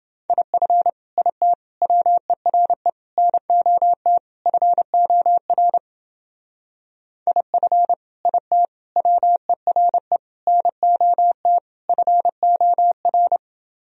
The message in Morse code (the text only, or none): IF IT WERE NOT FOR IF IT WERE NOT FOR